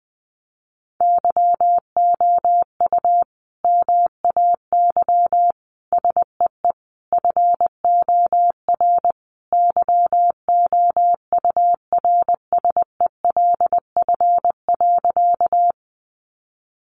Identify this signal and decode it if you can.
Morse code, heard as a keyed tone, about 20 wpm, 700 Hz. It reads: YOU MAY SEE FOR YOURSELF.
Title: Sentences from Top 600 Words 20wpm